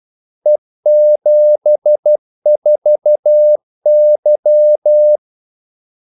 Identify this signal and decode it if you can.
Morse code, heard as a keyed tone, about 12 wpm, 600 Hz. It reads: E74Y
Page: 2 ekstra ordmellemrum mellem kaldesignalerne